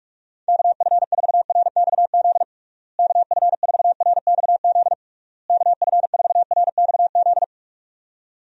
Morse code, heard as a keyed tone, about 46 wpm, 700 Hz. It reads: KF4RXZ KF4RXZ KF4RXZ
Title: USA Call Signs - 6 Characters - Repeated 3x - 45wpm